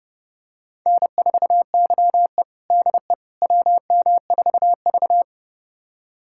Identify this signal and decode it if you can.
Morse code, heard as a keyed tone, about 32 wpm, 700 Hz. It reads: N4YE DE WM4V
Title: USA Callsign DE Callsign - 4 Characters - 30wpm